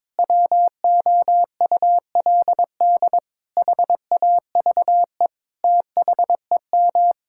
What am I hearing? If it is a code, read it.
Morse code, heard as a keyed tone, about 22 wpm, 700 Hz. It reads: WOULD HAVE THEM
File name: Sets of 9 Words - 22wpm